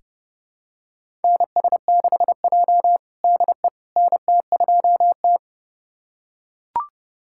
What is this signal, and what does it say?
Morse code, heard as a keyed tone, about 30 wpm, 700 Hz. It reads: NS6J DE NT2T
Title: USA Callsign DE Callsign - 4 Characters - 30wpm